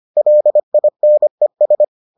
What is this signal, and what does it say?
Morse code, heard as a keyed tone, about 25 wpm, 600 Hz. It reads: LINES